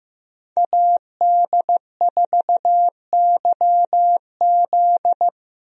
Morse code, heard as a keyed tone, about 15 wpm, 700 Hz. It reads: AD4YZ
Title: USA Call Signs - 5 Characters - 15wpm